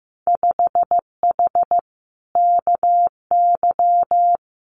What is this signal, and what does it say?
Morse code, heard as a keyed tone, about 15 wpm, 700 Hz. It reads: 5H KY
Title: Winter Field Day Exchanges Encore 15wpm